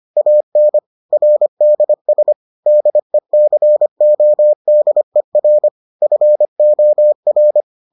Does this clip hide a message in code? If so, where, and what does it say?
Yes: Morse code from 0.2 to 7.6 s, heard as a keyed tone, about 25 wpm, 600 Hz. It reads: AN RDS DECODER FOR